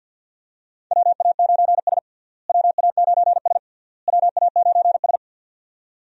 Morse code, heard as a keyed tone, about 52 wpm, 700 Hz. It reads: WA9S WA9S WA9S
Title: USA Call Signs - 4 Characters - Repeated 3x - 50wpm